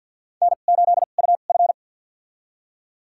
Morse code, heard as a keyed tone, about 47 wpm, spaced as for 35 wpm, 700 Hz. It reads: N8UF
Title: USA Call Signs - 4 Characters - 2x Character Spacing - 50wpm